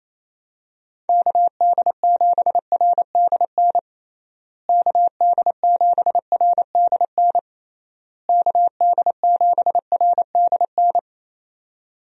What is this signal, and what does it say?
Morse code, heard as a keyed tone, about 27 wpm, 700 Hz. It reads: KD7RDN KD7RDN KD7RDN
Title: USA Call Signs - 6 Characters - Repeated 3x - 28wpm